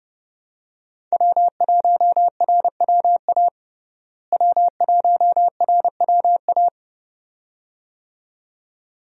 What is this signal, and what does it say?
Morse code, heard as a keyed tone, about 31 wpm, 700 Hz. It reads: W1RWA W1RWA
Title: USA Call Signs - 5 Characters - Repeated 2x - 30wpm